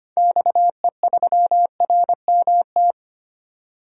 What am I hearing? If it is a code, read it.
Morse code, heard as a keyed tone, about 25 wpm, 700 Hz. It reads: XE3RMT
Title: DXCC-100 Call Signs - 6 Characters - 25wpm